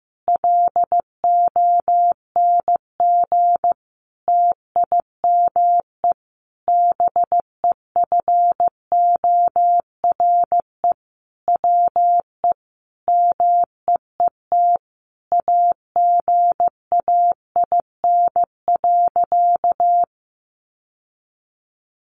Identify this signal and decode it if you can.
Morse code, heard as a keyed tone, about 15 wpm, 700 Hz. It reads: LONG TIME BEFORE WE MEET AGAIN.
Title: Sentences from Top 1100 Words 15wpm